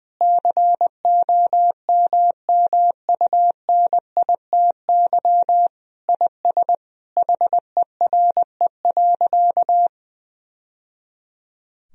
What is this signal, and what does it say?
Morse code, heard as a keyed tone, about 20 wpm, 700 Hz. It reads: COMMUNITY IS HERE.
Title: Sentences from Top 1000 Words - Review Part 4 - 20wpm